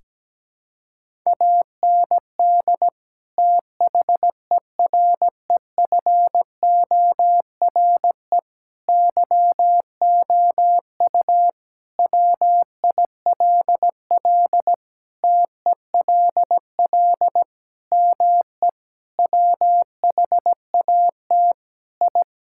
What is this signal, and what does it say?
Morse code, heard as a keyed tone, about 17 wpm, 700 Hz. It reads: AND THEREFORE YOU WILL TELL ME WHAT I
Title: Sentences from Top 1000 Words - Review Part 5 - 17wpm